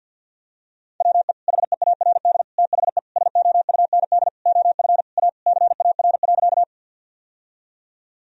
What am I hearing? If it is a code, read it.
Morse code, heard as a keyed tone, about 52 wpm, 700 Hz. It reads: WE HEARD THE SOUND OF A CAR.
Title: Sentences from Top 1000 Words - Review Part 1 - 50wpm